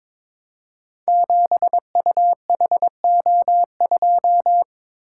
Morse code, heard as a keyed tone, about 22 wpm, 700 Hz. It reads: 7UHO2
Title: Code Groups - 5 Letter:Numbers - 22wpm